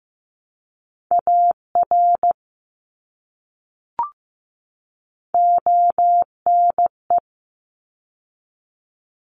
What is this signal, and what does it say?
Morse code, heard as a keyed tone, about 15 wpm, 700 Hz. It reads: AR ONE
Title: ARRL Sections 15wpm